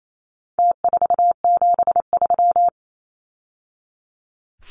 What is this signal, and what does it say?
Morse code, heard as a keyed tone, about 28 wpm, 700 Hz. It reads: T473
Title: Numbers - Contest Style - 4-Digit 28wpm